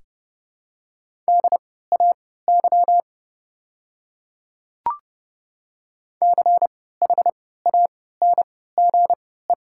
Morse code, heard as a keyed tone, about 31 wpm, spaced as for 17 wpm, 700 Hz. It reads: DAY CHANGE
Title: 3x Character Spacing - Top 200 Words - 30wpm